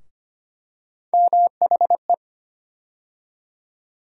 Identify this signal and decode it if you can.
Morse code, heard as a keyed tone, about 25 wpm, 700 Hz. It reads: MHE
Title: USA Call Sign Postfixes - 3 Letters 25wpm